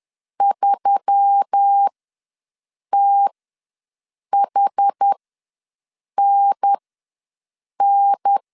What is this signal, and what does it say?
Morse code, heard as a keyed tone, about 10 wpm, spaced as for 6 wpm, 800 Hz. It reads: 3THNN
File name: LEZIONI / 5 LBVC / 5 L ws 10wpm 3636